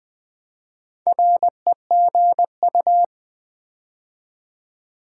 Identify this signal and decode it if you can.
Morse code, heard as a keyed tone, about 20 wpm, 700 Hz. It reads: REGU
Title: Build A Word - 4 Letters At A Time - 10-Letter Words - 20wpm